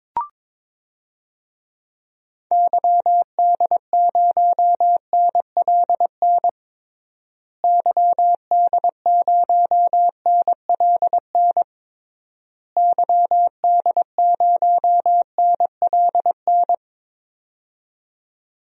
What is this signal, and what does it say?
Morse code, heard as a keyed tone, about 22 wpm, 700 Hz. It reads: YD0NLN YD0NLN YD0NLN
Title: DXCC-100 Call Signs - 6 Characters - Repeated 3x - 22wpm